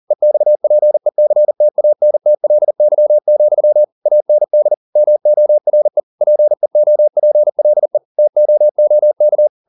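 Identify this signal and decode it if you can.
Morse code, heard as a keyed tone, about 40 wpm, 600 Hz. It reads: EXPECTANTLY, AND MORE PEOPLE TOOK
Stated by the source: Morse code at 40 wpm, practice text, CW med Gnister.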